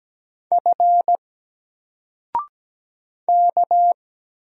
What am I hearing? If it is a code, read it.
Morse code, heard as a keyed tone, about 17 wpm, 700 Hz. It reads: F K
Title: Single Letter-Number 17wpm